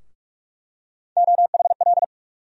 Morse code, heard as a keyed tone, about 47 wpm, 700 Hz. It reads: OHL
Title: USA Call Sign Postfixes - 3 Letters 45wpm